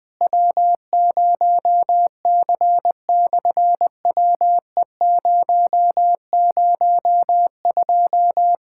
Morse code, heard as a keyed tone, about 20 wpm, 700 Hz. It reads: W0C/WE002
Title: SOTA QSO Practice 20wpm